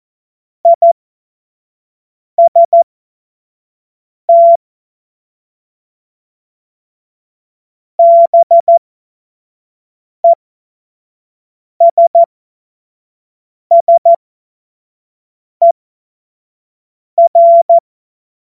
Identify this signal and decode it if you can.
Morse code, heard as a keyed tone, about 14 wpm, spaced as for 5 wpm, 680 Hz. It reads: IST BESSER
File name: Morsekurs / Witze / Witz 21. Dezember 2021, 50 F70